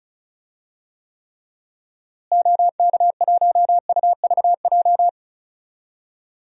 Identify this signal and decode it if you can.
Morse code, heard as a keyed tone, about 34 wpm, 700 Hz. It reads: OK1UVJ
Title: DXCC-100 Call Signs - 6 Characters - Repeated 2x - 35wpm